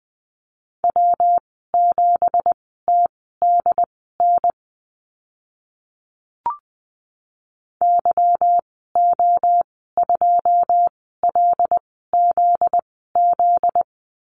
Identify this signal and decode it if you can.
Morse code, heard as a keyed tone, about 20 wpm, spaced as for 14 wpm, 700 Hz. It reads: W7TDN YO2LZZ
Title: 245 Lesson 39g - Letter z - Call Signs 20wpm